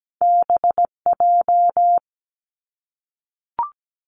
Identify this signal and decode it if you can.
Morse code, heard as a keyed tone, about 17 wpm, 700 Hz. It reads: BJ
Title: Top Two Letter Combinations in Words 17wpm